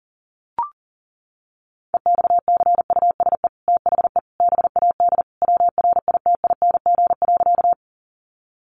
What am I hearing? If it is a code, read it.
Morse code, heard as a keyed tone, about 37 wpm, 700 Hz. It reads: EXCUSE THE BAD WRITING.
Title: Sentences from Top 800 Words 40wpm